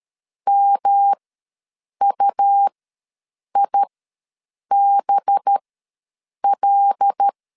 Morse code, heard as a keyed tone, about 13 wpm, spaced as for 7 wpm, 800 Hz. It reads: MUIBL
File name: 5 LB ws 12wpm